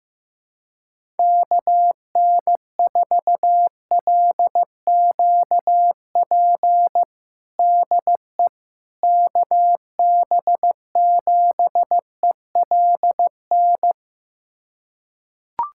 Morse code, heard as a keyed tone, about 15 wpm, 700 Hz. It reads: KN4LQP DE KB7ELN
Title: USA Callsign DE Callsign 15wpm